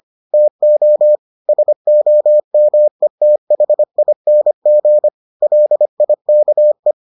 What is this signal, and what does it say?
Morse code, heard as a keyed tone, about 25 wpm, 600 Hz. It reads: TO SOMETHING LIKE